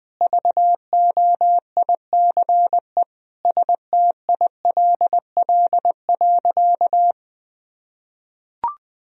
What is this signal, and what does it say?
Morse code, heard as a keyed tone, about 21 wpm, 700 Hz. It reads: VOICE STILL.
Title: Sentences from Top 500 Words and Abbreviations 20wpm